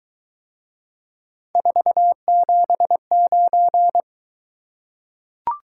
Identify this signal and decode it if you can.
Morse code, heard as a keyed tone, about 23 wpm, 700 Hz. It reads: 479